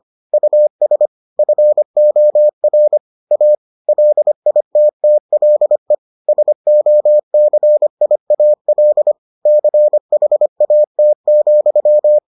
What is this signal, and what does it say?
Morse code, heard as a keyed tone, about 25 wpm, 600 Hz. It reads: US FOR A LITTLE SOCIAL CHAT,